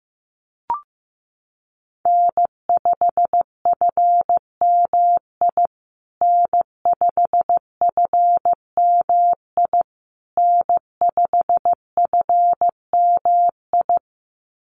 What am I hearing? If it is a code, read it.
Morse code, heard as a keyed tone, about 15 wpm, 700 Hz. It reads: N5FMI N5FMI N5FMI
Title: USA Call Signs Repeated 3x 15wpm